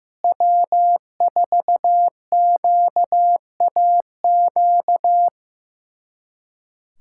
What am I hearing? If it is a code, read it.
Morse code, heard as a keyed tone, about 15 wpm, 700 Hz. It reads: W4QAQ
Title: USA Callsign DE Callsign - 5 Characters - 15wpm